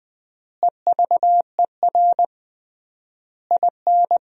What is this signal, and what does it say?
Morse code, heard as a keyed tone, about 20 wpm, 700 Hz. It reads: EVER IN
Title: 3x Word-Spacing - Sentences from Top 400 Words - 20wpm